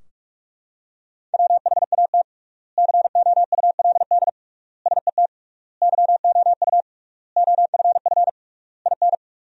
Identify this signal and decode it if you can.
Morse code, heard as a keyed tone, about 41 wpm, 700 Hz. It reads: WHAT COULD SET YOU OFF IN
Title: 3x Word-Spacing - Sentences from Top 200 Words - 45wpm